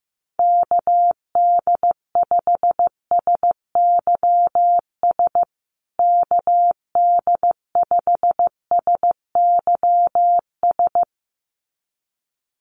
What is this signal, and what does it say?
Morse code, heard as a keyed tone, about 15 wpm, 700 Hz. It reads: KD5SYS KD5SYS
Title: USA Call Signs Repeated 2x 15wpm